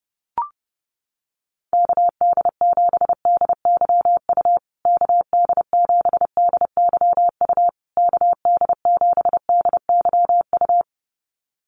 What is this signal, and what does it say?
Morse code, heard as a keyed tone, about 30 wpm, 700 Hz. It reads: KD7DYU KD7DYU KD7DYU
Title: USA Call Signs Repeated 3x 30wpm